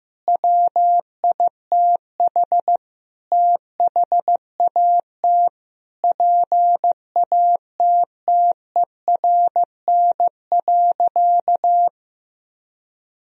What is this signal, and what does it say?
Morse code, heard as a keyed tone, about 15 wpm, 700 Hz. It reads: WITH THAT PATTERN.
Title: Sentences from Top 1000 Words - Review Part 2 - 15wpm